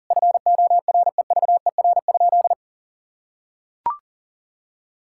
Morse code, heard as a keyed tone, about 40 wpm, 700 Hz. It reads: FOREVER?
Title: Sentences from Top 1600 Words 40wpm